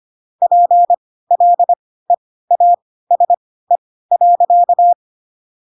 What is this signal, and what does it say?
Morse code, heard as a keyed tone, about 25 wpm, spaced as for 16 wpm, 700 Hz. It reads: PLEASE.
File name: Pinocchio / Pinocchio ch4 10 25 ik1hge